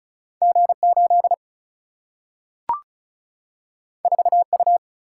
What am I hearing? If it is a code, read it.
Morse code, heard as a keyed tone, about 36 wpm, 700 Hz. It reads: G8 4U
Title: Single-Letter Single-Number 35wpm